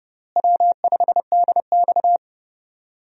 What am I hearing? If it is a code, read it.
Morse code, heard as a keyed tone, about 31 wpm, 700 Hz. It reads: W5DX
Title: USA Call Signs - 4 Characters - Repeated 2x - 30wpm